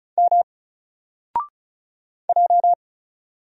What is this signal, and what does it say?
Morse code, heard as a keyed tone, about 36 wpm, 700 Hz. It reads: M J
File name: Single-Letter-Number-35wpm